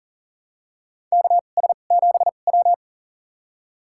Morse code, heard as a keyed tone, about 40 wpm, spaced as for 29 wpm, 700 Hz. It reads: KS7W
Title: USA Call Signs - 2x Character Spacing - 40wpm